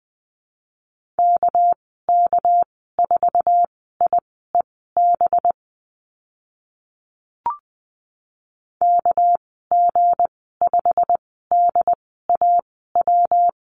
Morse code, heard as a keyed tone, about 20 wpm, spaced as for 14 wpm, 700 Hz. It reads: KK4IEB KG5DAW
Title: 231 Lesson 37g - Letter x - Call Signs 20wpm